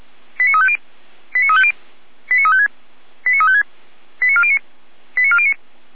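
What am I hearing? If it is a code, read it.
Five-tone selective call (ZVEI), heard as quick runs of five tones, tones of about 70 ms. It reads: 89350 89350 89356 89356 89409 89409
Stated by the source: ZVEI five-tone selective call 89104, received by SDR, from Feuerwehr Orlamünde fire brigade alerting archive.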